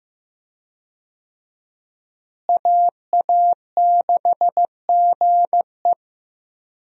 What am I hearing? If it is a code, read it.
Morse code, heard as a keyed tone, about 15 wpm, 700 Hz. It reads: AA6GE
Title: USA Call Signs - 5 Characters - Repeated 3x - 15wpm